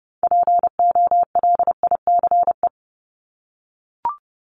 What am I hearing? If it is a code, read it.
Morse code, heard as a keyed tone, about 30 wpm, 700 Hz. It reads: POLICE
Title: Top 1751-2000 Words 30wpm